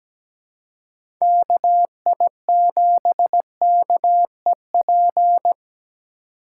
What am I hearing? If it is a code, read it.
Morse code, heard as a keyed tone, about 17 wpm, 700 Hz. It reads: KI7KEP
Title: USA Call Signs - 6 Characters - 4x Character Spacing - 17wpm